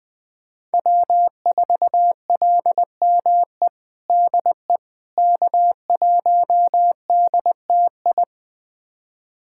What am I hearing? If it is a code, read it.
Morse code, heard as a keyed tone, about 21 wpm, 700 Hz. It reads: W4LME DE K1DTI
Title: USA Callsign DE Callsign - 5 Characters - 20wpm